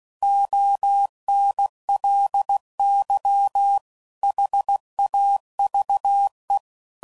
Morse code, heard as a keyed tone, about 16 wpm, 800 Hz. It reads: ONLY HAVE
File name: CW Code Groups @15WPM / Technical Subjects / AND Gate